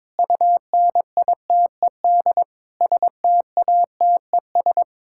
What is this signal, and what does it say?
Morse code, heard as a keyed tone, about 22 wpm, 700 Hz. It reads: UNITED STATES
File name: practice / 22wpm / Sentences with 4-Word Phrases - 22wpm